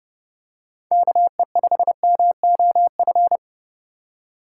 Morse code, heard as a keyed tone, about 30 wpm, 700 Hz. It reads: KE5MOF